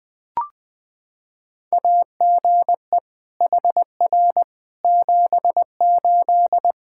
Morse code, heard as a keyed tone, about 20 wpm, 700 Hz. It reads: AGE HR 78
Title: K1USN SST QSO Practice 20wpm